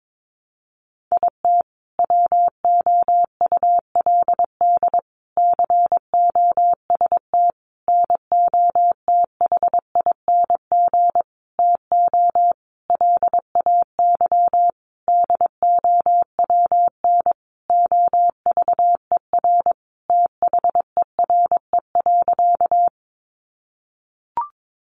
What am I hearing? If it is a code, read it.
Morse code, heard as a keyed tone, about 22 wpm, 700 Hz. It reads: IT WOULD COST NOTHING TO LAY DOWN OVER THERE.
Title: Sentences from Top 800 Words 22wpm